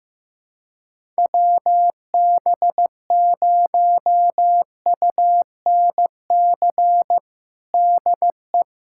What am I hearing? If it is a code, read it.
Morse code, heard as a keyed tone, about 15 wpm, 700 Hz. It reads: WB0UNC DE
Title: USA Callsign DE Callsign 15wpm